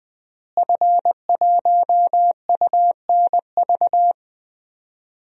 Morse code, heard as a keyed tone, about 21 wpm, 700 Hz. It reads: F1UNV
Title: DXCC-100 Call Signs - 5 Characters - Repeated 3x - 20wpm